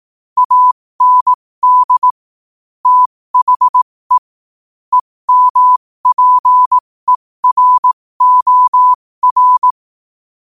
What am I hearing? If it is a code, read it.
Morse code, heard as a keyed tone, about 18 wpm, spaced as for 15 wpm, 1000 Hz. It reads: AND THE EMPEROR
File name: 15wpm